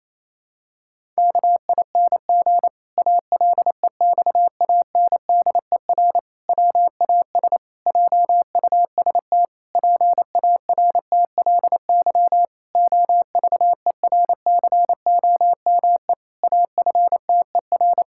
Morse code, heard as a keyed tone, about 28 wpm, 700 Hz. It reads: KING ALEXANDER WAS JUST PARTLY OVERCOME AFTER
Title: Pangram Challenge v2 - 28wpm